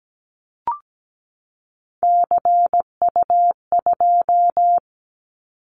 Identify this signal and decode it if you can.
Morse code, heard as a keyed tone, about 17 wpm, 700 Hz. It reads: CU2
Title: Callsign Prefixes for Top 50 DXCC - 17wpm